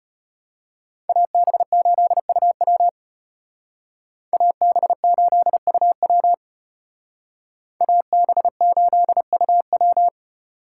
Morse code, heard as a keyed tone, about 32 wpm, 700 Hz. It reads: AB8UW AB8UW AB8UW